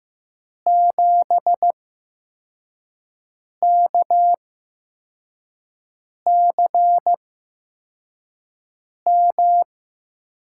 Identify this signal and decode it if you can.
Morse code, heard as a keyed tone, about 15 wpm, spaced as for 4 wpm, 700 Hz. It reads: 7KCM
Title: USA Call Signs - 6 Characters - 8x Character Spacing - 15wpm